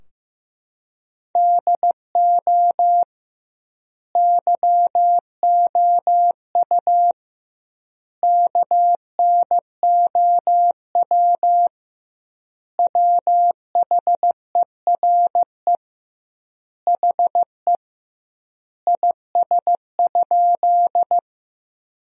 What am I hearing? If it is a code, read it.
Morse code, heard as a keyed tone, about 15 wpm, 700 Hz. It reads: DO YOU KNOW WHERE HE IS?
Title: 2x Word-Spacing - Sentences from Top 300 Words - 15wpm